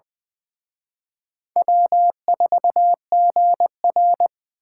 Morse code, heard as a keyed tone, about 24 wpm, 700 Hz. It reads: W4GR
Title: Speed Racing - USA 1x2 Call Signs 20wpm